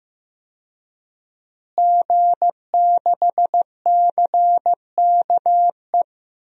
Morse code, heard as a keyed tone, about 15 wpm, 700 Hz. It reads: G6CKE
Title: DXCC-100 Call Signs - 5 Characters - Repeated 2x - 15wpm